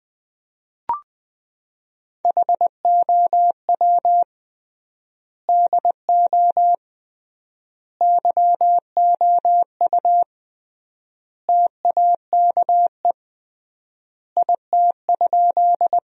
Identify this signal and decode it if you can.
Morse code, heard as a keyed tone, about 20 wpm, 700 Hz. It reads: HOW DO YOU TAKE IT?
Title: 3x Word-Spacing - Sentences from Top 200 Words - 20wpm